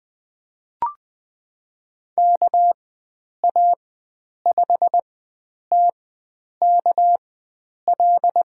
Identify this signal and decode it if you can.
Morse code, heard as a keyed tone, about 20 wpm, spaced as for 9 wpm, 700 Hz. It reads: KA5TKL
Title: USA Call Signs - 4x Character Spacing - 20wpm